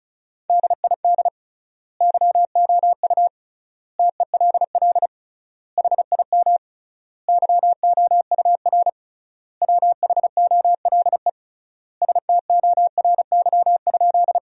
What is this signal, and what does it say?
Morse code, heard as a keyed tone, about 34 wpm, 700 Hz. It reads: DID YOU TELL HIM YOUR WHOLE STORY?
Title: 3x Word-Spacing - Sentences from Top 400 Words - 35wpm